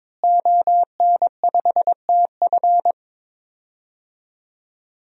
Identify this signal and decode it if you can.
Morse code, heard as a keyed tone, about 22 wpm, 700 Hz. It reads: ON5TF
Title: DXCC-100 Call Signs - 5 Characters - Repeated 2x - 22wpm